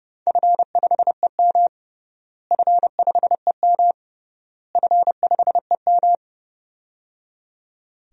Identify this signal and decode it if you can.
Morse code, heard as a keyed tone, about 30 wpm, 700 Hz. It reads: F5EM F5EM F5EM
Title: DXCC-100 Call Signs - 4 Characters - Repeated 3x - 30wpm